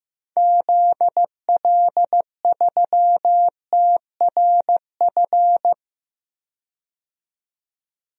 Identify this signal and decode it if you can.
Morse code, heard as a keyed tone, about 15 wpm, 700 Hz. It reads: ZL3TRF
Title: DXCC-100 Call Signs - 6 Characters - 15wpm